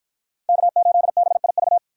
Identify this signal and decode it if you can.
Morse code, heard as a keyed tone, about 52 wpm, 700 Hz. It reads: K8BIV